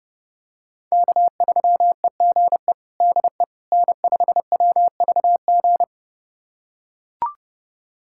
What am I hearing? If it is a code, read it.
Morse code, heard as a keyed tone, about 30 wpm, 700 Hz. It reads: K3EGE DE N5WVG
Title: USA Callsign DE Callsign - 5 Characters - 30wpm